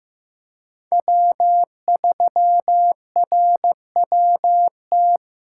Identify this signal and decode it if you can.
Morse code, heard as a keyed tone, about 15 wpm, 700 Hz. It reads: W3RWT
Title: USA Callsign DE Callsign - 5 Characters - 15wpm